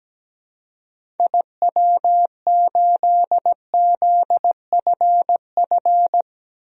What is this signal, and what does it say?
Morse code, heard as a keyed tone, about 17 wpm, 700 Hz. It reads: IW8ZFF
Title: DXCC-100 Call Signs - 6 Characters - 17wpm